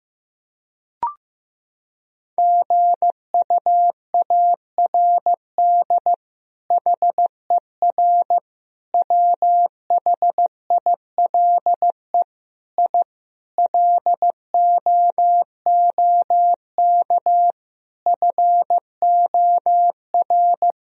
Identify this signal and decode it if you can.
Morse code, heard as a keyed tone, about 16 wpm, 700 Hz. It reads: GUARD HER WHILE I LOOK FOR
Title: Sentences from Top 1000 Words - Review Part 4 - 15wpm